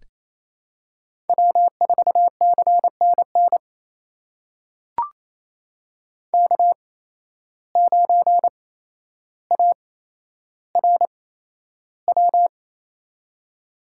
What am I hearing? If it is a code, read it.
Morse code, heard as a keyed tone, about 29 wpm, spaced as for 12 wpm, 700 Hz. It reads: W4CNN K9ARW
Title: USA Call Signs - 5 Characters - 8x Character Spacing - 28wpm